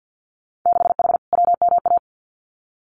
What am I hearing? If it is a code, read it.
Morse code, heard as a keyed tone, about 50 wpm, 700 Hz. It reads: 6H WMA
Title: Winter Field Day Exchanges Encore 50wpm